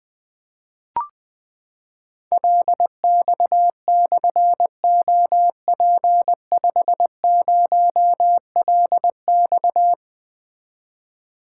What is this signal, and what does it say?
Morse code, heard as a keyed tone, about 20 wpm, 700 Hz. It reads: LX/OP50LX
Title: Difficult Call Signs 20wpm